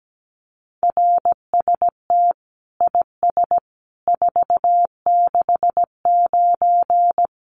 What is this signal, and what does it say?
Morse code, heard as a keyed tone, about 17 wpm, 700 Hz. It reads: RST IS 469
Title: Common QSO Phrases 17wpm